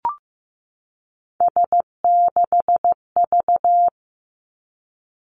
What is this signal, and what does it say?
Morse code, heard as a keyed tone, about 15 wpm, 700 Hz. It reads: S6V
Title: Three Challenging Letter-Number Combinations 15wpm